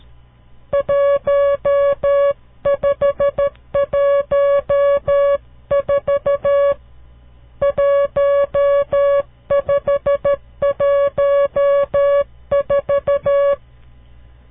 Morse code, heard as a keyed tone, about 13 wpm, 560 Hz. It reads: 1514 1514
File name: CW Learning Programs / farnsworth / unit 5